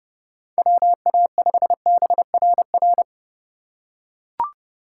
Morse code, heard as a keyed tone, about 30 wpm, 700 Hz. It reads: WA5BRR
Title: USA Call Signs - 6 Characters - 3x Character Spacing - 30wpm